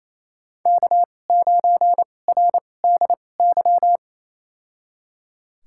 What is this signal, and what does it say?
Morse code, heard as a keyed tone, about 28 wpm, spaced as for 20 wpm, 700 Hz. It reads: K9RDY
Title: USA Call Signs - 2x Character Spacing - 28wpm